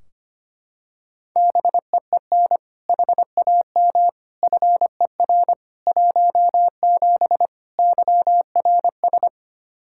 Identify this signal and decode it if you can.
Morse code, heard as a keyed tone, about 25 wpm, 700 Hz. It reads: BEEN HAM FER 17 YRS
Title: Practice QSO - Exchange 3 - More Info - 25wpm